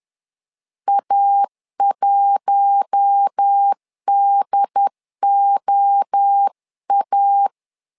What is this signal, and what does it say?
Morse code, heard as a keyed tone, about 10 wpm, 800 Hz. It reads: A1DOA
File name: LEZIONI / 4 Q1 10wpm 2448